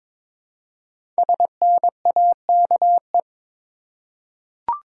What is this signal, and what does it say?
Morse code, heard as a keyed tone, about 22 wpm, 700 Hz. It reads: SNAKE
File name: 5 Letter Words - First 3 Letters Given - 22wpm